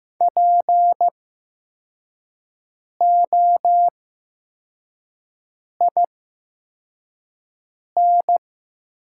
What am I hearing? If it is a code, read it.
Morse code, heard as a keyed tone, about 15 wpm, spaced as for 4 wpm, 700 Hz. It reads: POIN
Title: 8x Character Spacing - Top 300 Words - 15wpm